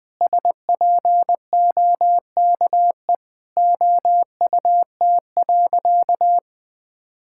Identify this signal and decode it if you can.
Morse code, heard as a keyed tone, about 20 wpm, 700 Hz. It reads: SPOKE OUT.
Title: Sentences with 7-Word Phrases - Spoken Beginning - 20wpm